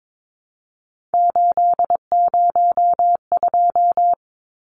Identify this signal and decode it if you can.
Morse code, heard as a keyed tone, about 22 wpm, 700 Hz. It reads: 802
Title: Numbers - 3-Digit 22wpm